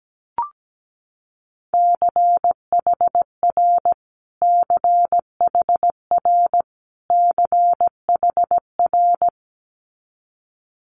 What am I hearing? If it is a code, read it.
Morse code, heard as a keyed tone, about 17 wpm, 690 Hz. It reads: CHR CHR CHR
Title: Multi-letter Phonemes Repeated 3x 17wpm